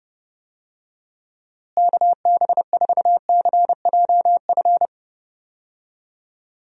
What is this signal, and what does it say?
Morse code, heard as a keyed tone, about 30 wpm, 700 Hz. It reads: KB4CJF
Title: USA Call Signs - 6 Characters - Repeated 2x - 30wpm